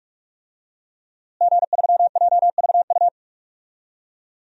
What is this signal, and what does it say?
Morse code, heard as a keyed tone, about 46 wpm, 700 Hz. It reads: G3JVU
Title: DXCC-100 Call Signs - 5 Characters - Repeated 2x - 45wpm